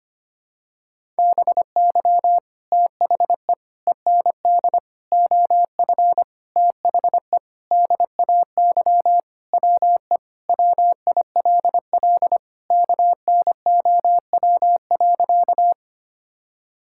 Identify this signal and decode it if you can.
Morse code, heard as a keyed tone, about 25 wpm, 700 Hz. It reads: BY THE END OF THE DAY WE WILL KNOW.
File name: practice / 25wpm / Sentences with 4-Word Phrases - 25wpm